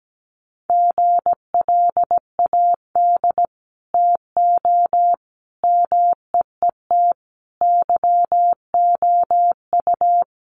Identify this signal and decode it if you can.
Morse code, heard as a keyed tone, about 17 wpm, 700 Hz. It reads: GLAD TO MEET YOU
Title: Sentences from Top 1100 Words 17wpm